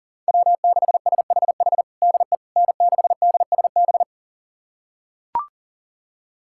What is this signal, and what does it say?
Morse code, heard as a keyed tone, about 40 wpm, 700 Hz. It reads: W6SHH DE N6DSB
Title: USA Callsign DE Callsign - 5 Characters - 40wpm